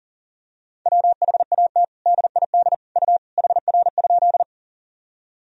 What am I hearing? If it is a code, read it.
Morse code, heard as a keyed tone, about 40 wpm, 700 Hz. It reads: WHAT DID U HR?
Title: Sentences from Top 300 Words and Abbreviations 40wpm